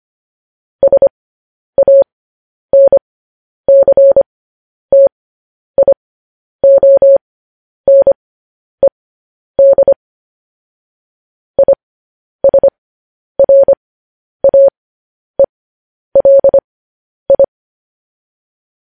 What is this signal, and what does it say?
Morse code, heard as a keyed tone, about 25 wpm, spaced as for 10 wpm, 550 Hz. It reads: SANCTIONED ISRAELI
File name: podcast / News headlines-1x-25x10